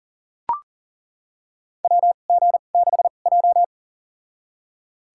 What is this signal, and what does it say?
Morse code, heard as a keyed tone, about 38 wpm, spaced as for 28 wpm, 700 Hz. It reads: WG6J
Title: USA Call Signs - 4 Characters - 2x Character Spacing - 40wpm